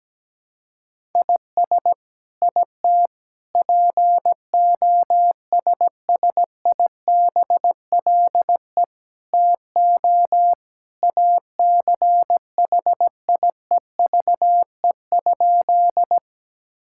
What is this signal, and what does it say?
Morse code, heard as a keyed tone, about 17 wpm, 700 Hz. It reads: IS IT POSSIBLE TO ACHIEVE?
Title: Sentences from Top 1000 Words - Review Part 6 - 17wpm